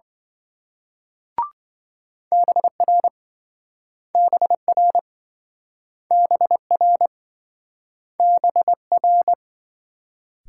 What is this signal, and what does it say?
Morse code, heard as a keyed tone, about 26 wpm, 700 Hz. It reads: BR BR BR BR